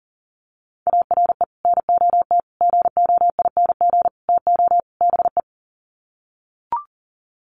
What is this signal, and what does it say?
Morse code, heard as a keyed tone, about 40 wpm, 700 Hz. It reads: ARE NOT GOING TO BE
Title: Sets of 5 Words Encore 40wpm